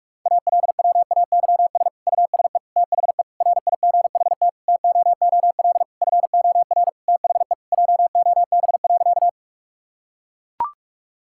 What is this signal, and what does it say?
Morse code, heard as a keyed tone, about 41 wpm, 700 Hz. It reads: ALWAYS USE THE RIGHT TOOL FOR THE JOB.
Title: Sentences from Top 1000 Words - Review Part 6 - 45wpm